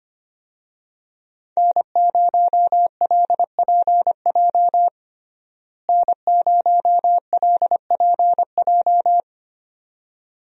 Morse code, heard as a keyed tone, about 25 wpm, 700 Hz. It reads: N0LPJ N0LPJ
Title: USA Call Signs - 5 Characters - Repeated 2x - 25wpm